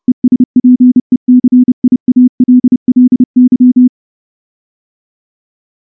Morse code, heard as a keyed tone, about 29 wpm, 250 Hz. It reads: ESPECIALLY
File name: morse / morse-training-30WPM